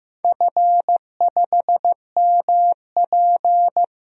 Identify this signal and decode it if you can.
Morse code, heard as a keyed tone, about 15 wpm, 700 Hz. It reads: F5MP
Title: DXCC-100 Call Signs - 4 Characters - Repeated 2x - 15wpm